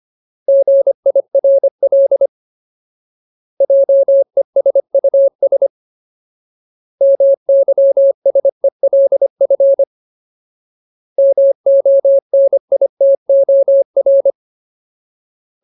Morse code, heard as a keyed tone, about 25 wpm, 550 Hz. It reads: GIRL JESUS MYSELF MONITOR
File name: Ultimate Word List / 25WPM Ultimate